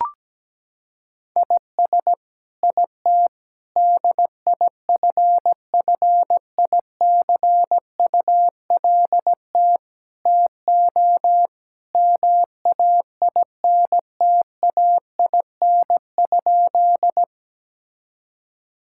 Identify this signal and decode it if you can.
Morse code, heard as a keyed tone, about 17 wpm, 700 Hz. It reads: IS IT DIFFICULT TO MAINTAIN?
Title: Sentences from Top 1000 Words - Review Part 5 - 17wpm